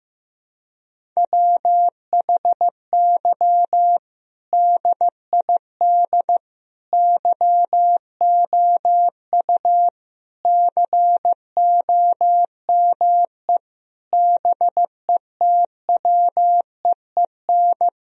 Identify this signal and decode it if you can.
Morse code, heard as a keyed tone, about 15 wpm, 700 Hz. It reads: WHY DID YOU COME BETWEEN
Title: Sentences from Top 1000 Words - Review Part 1 - 15wpm